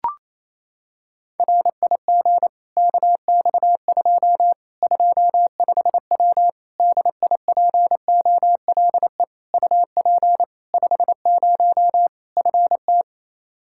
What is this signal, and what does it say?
Morse code, heard as a keyed tone, about 27 wpm, 700 Hz. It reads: RIG KX2 25W DIPOLE UP 50 FT
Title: K1USN SST QSO Practice 28wpm